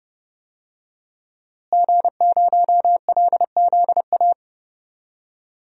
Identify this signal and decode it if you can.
Morse code, heard as a keyed tone, about 30 wpm, 700 Hz. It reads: G0LZA
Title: DXCC-100 Call Signs - 5 Characters - Repeated 2x - 30wpm